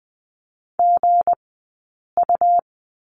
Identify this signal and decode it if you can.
Morse code, heard as a keyed tone, about 20 wpm, 700 Hz. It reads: G U
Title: 190 Lesson 31d - Letter v - Two Characters 20wpm